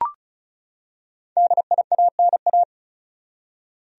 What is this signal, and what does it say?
Morse code, heard as a keyed tone, about 36 wpm, 700 Hz. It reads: DIANA
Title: USA Common Names - Max 5 Letters - 35wpm